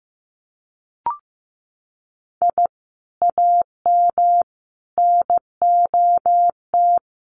Morse code, heard as a keyed tone, about 15 wpm, 700 Hz. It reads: I AM NOT
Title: Sentences from Top 600 Words 15wpm